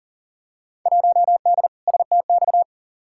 Morse code, heard as a keyed tone, about 40 wpm, 700 Hz. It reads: 1D STX
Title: Field Day Exchanges - 40wpm